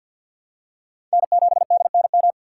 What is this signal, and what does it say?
Morse code, heard as a keyed tone, about 54 wpm, 700 Hz. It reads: N7DNM
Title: USA Call Signs - 5 Characters - 6x Character Spacing - 50wpm